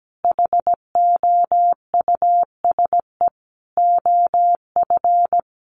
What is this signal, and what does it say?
Morse code, heard as a keyed tone, about 17 wpm, 700 Hz. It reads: HOUSE OF
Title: Sets of 5 Words Encore 17wpm